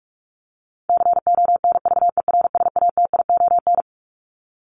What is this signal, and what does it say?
Morse code, heard as a keyed tone, about 44 wpm, 700 Hz. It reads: CONVERSATION
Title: Top 1751-2000 Words 45wpm